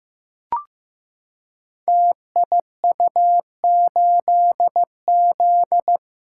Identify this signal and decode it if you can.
Morse code, heard as a keyed tone, about 15 wpm, 700 Hz. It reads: TIU8Z
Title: Code Groups - 5 Letter:Numbers - 15wpm